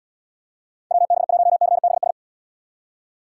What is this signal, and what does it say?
Morse code, heard as a keyed tone, about 78 wpm, 700 Hz. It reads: WB1CBS